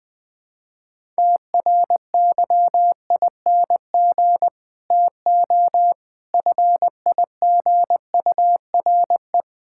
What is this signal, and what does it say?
Morse code, heard as a keyed tone, about 20 wpm, 700 Hz. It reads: TRYING TO FIGURE
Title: Build A Phrase - 4-Word Phrases - 20wpm